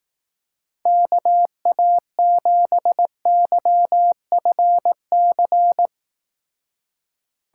Morse code, heard as a keyed tone, about 18 wpm, 700 Hz. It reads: KA7YFC
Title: Speed Racing - USA 2x3 Call Signs 15wpm